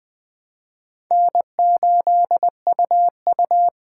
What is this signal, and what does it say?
Morse code, heard as a keyed tone, about 20 wpm, 700 Hz. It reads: N8UU
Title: USA Call Signs - 4 Characters - 3x Character Spacing - 20wpm